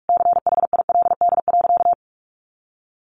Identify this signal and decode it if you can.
Morse code, heard as a keyed tone, about 44 wpm, 700 Hz. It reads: CHILD.
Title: Sentences from Top 1300 Words 45wpm